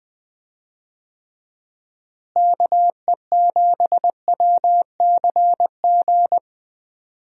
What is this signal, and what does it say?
Morse code, heard as a keyed tone, about 20 wpm, 700 Hz. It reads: KE7WCG
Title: USA Call Signs - 6 Characters - Repeated 3x - 20wpm